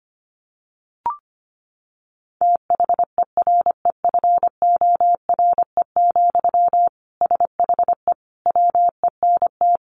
Morse code, heard as a keyed tone, about 25 wpm, 700 Hz. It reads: THEREFORE, SHE WENT
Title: Sentences from Top 800 Words 25wpm